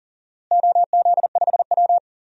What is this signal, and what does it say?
Morse code, heard as a keyed tone, about 37 wpm, 700 Hz. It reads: OZ5W
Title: DXCC-100 Call Signs - 4 Characters - Repeated 2x - 40wpm